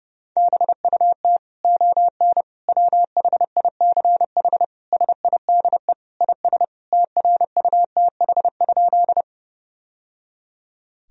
Morse code, heard as a keyed tone, about 30 wpm, 700 Hz. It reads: BUT ON WHICH SIDE IS TRUTH?
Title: Sentences from Top 1000 Words - Review Part 1 - 30wpm